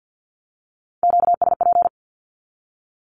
Morse code, heard as a keyed tone, about 50 wpm, 700 Hz. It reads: QSP
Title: Q-Codes with Spoken Meaning - 50wpm